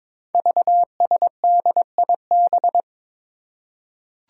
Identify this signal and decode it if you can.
Morse code, heard as a keyed tone, about 23 wpm, 700 Hz. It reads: VSDIB
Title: Five Challenging Letter-Number Combinations 22wpm